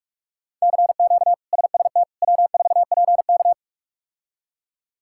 Morse code, heard as a keyed tone, about 46 wpm, 700 Hz. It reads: CQ SST W4PK
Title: K1USN SST QSO Practice - 45wpm